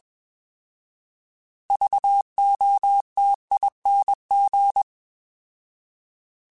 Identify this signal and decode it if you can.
Morse code, heard as a keyed tone, about 21 wpm, 800 Hz. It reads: VOTING
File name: Alphabetical / Alphabetical mp3s / Vs @20